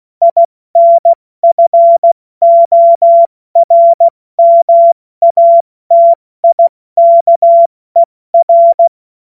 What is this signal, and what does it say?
Morse code, heard as a keyed tone, about 16 wpm, 680 Hz. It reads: INFORMATIKER
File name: Morsekurs / Witze / Witz 14. Dezember 2021, 70 F80